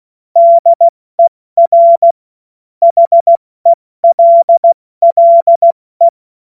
Morse code, heard as a keyed tone, about 16 wpm, 680 Hz. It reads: DER HELLE
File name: Morsekurs / Witze / Witz 02 April 2024 70 F80